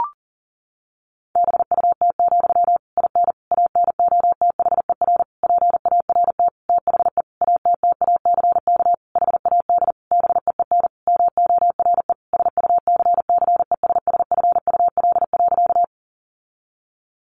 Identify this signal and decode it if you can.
Morse code, heard as a keyed tone, about 40 wpm, 700 Hz. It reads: BUT, IN ANOTHER PART THE ATTACK HAD BEEN MORE SUCCESSFUL.
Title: Sentences from Top 1500 Words 40wpm